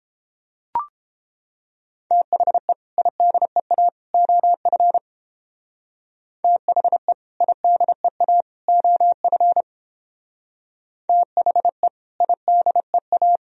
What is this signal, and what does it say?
Morse code, heard as a keyed tone, about 29 wpm, 700 Hz. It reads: THE IDEA OF THE IDEA OF THE IDEA